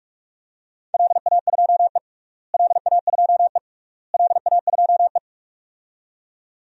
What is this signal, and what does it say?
Morse code, heard as a keyed tone, about 44 wpm, 700 Hz. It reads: LA2E LA2E LA2E
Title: DXCC-100 Call Signs - 4 Characters - Repeated 3x - 45wpm